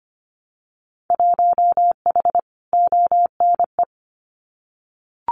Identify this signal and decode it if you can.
Morse code, heard as a keyed tone, about 25 wpm, 700 Hz. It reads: 1H ONE
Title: Winter Field Day Exchanges 25wpm